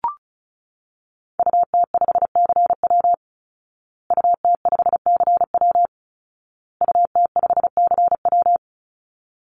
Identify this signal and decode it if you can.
Morse code, heard as a keyed tone, about 34 wpm, 700 Hz. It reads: UT5CW UT5CW UT5CW
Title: Call Signs Repeated 2x - 35wpm